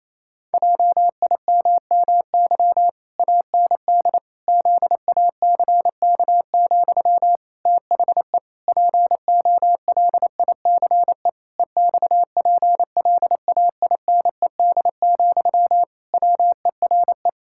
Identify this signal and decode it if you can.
Morse code, heard as a keyed tone, about 27 wpm, 700 Hz. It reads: JIMMY AND ZACK, THE POLICE EXPLAINED, WERE
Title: Pangram Challenge v2 - 28wpm